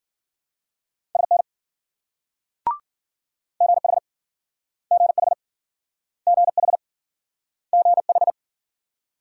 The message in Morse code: SN GH GH GH GH

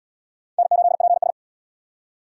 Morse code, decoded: N8PS

